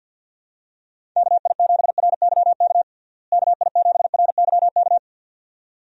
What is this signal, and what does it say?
Morse code, heard as a keyed tone, about 52 wpm, 700 Hz. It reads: KI7RYK KI7RYK
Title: USA Call Signs - 6 Characters - Repeated 2x - 50wpm